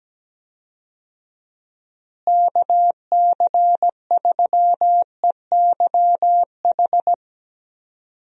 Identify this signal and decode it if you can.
Morse code, heard as a keyed tone, about 17 wpm, 700 Hz. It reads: KC3EYH
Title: USA Call Signs - 6 Characters - Repeated 2x - 17wpm